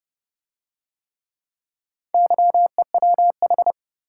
Morse code, heard as a keyed tone, about 29 wpm, 700 Hz. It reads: YEWH